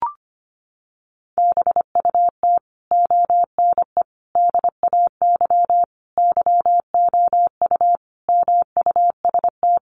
Morse code, heard as a keyed tone, about 25 wpm, 700 Hz. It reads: BUT ONE DAY YOU MUST
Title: Sentences from Top 400 Words 25wpm